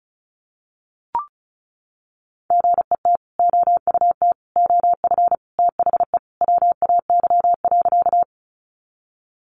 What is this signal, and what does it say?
Morse code, heard as a keyed tone, about 34 wpm, 700 Hz. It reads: GET OUT OF THE WAY.
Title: Sentences from Top 200 Words 35wpm